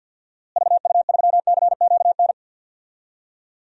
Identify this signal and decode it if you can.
Morse code, heard as a keyed tone, about 46 wpm, 700 Hz. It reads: VU3CQN
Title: DXCC-100 Call Signs - 6 Characters - Repeated 3x - 50wpm